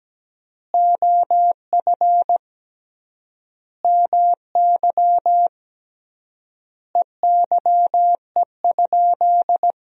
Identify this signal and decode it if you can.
Morse code, heard as a keyed tone, about 17 wpm, 700 Hz. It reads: OF MY EYE?
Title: 3x Word-Spacing - Sentences from Top 500 Words - 17wpm